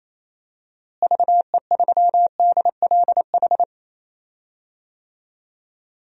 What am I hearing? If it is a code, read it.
Morse code, heard as a keyed tone, about 28 wpm, 700 Hz. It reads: VE3DLH
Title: DXCC-100 Call Signs - 6 Characters - Repeated 2x - 28wpm